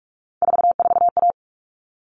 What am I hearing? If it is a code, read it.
Morse code, heard as a keyed tone, about 44 wpm, 700 Hz. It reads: 44A
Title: All Cut Numbers - 3-Digit 45wpm